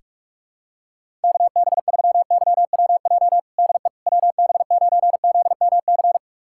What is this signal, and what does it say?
Morse code, heard as a keyed tone, about 41 wpm, 690 Hz. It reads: KB3YWJ DE WB9ZMC
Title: USA Callsign DE Callsign 45wpm